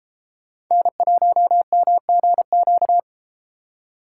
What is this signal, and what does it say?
Morse code, heard as a keyed tone, about 33 wpm, 700 Hz. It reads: N1MGQ